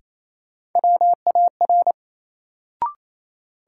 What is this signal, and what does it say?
Morse code, heard as a keyed tone, about 28 wpm, 700 Hz. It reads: WAR
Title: Sets of 5 Words - New Word Drill - 28wpm